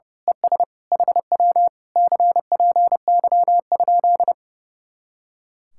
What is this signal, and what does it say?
Morse code, heard as a keyed tone, about 29 wpm, 700 Hz. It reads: ES HW CPY?
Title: Speed-Racing Common QSO Phrases - 30wpm